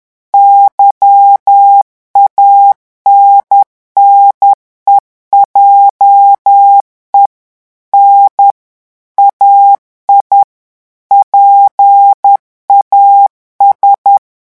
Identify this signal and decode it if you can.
Morse code, heard as a keyed tone, about 10 wpm, 800 Hz. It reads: YANNE JE N AI PAS
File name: revision9 citations humoristiques vitesse 10 mots minute